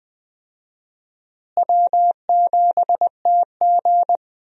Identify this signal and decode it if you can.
Morse code, heard as a keyed tone, about 21 wpm, 700 Hz. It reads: W7TG
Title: USA Call Signs - 4 Characters - Repeated 2x - 20wpm